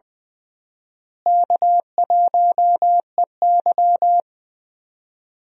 Morse code, heard as a keyed tone, about 20 wpm, 700 Hz. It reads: K1EY
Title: USA Call Signs - 4 Characters - Repeated 2x - 20wpm